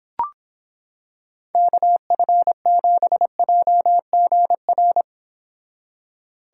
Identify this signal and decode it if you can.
Morse code, heard as a keyed tone, about 25 wpm, 700 Hz. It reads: KF7JGR